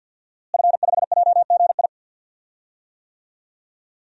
Morse code, heard as a keyed tone, about 48 wpm, 700 Hz. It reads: F5JGI
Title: DXCC-100 Call Signs - 5 Characters - Repeated 2x - 50wpm